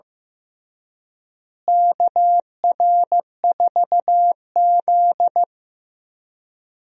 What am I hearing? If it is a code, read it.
Morse code, heard as a keyed tone, about 15 wpm, 700 Hz. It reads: KR4Z
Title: USA Call Signs - 4 Characters - Repeated 3x - 15wpm